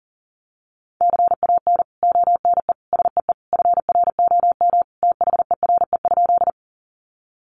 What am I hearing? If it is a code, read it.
Morse code, heard as a keyed tone, about 40 wpm, 700 Hz. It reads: CAN ONE SEE FROM THERE?
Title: Sentences from Top 100 Words 40wpm